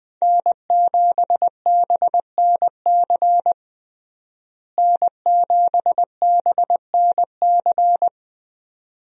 Morse code, heard as a keyed tone, about 20 wpm, 700 Hz. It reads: N7BNC N7BNC
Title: USA Call Signs - 5 Characters - Repeated 2x - 20wpm